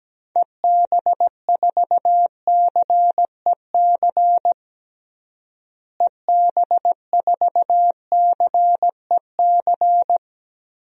Morse code, heard as a keyed tone, about 17 wpm, 700 Hz. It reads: EB4CEC EB4CEC
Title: DXCC-100 Call Signs - 6 Characters - Repeated 3x - 17wpm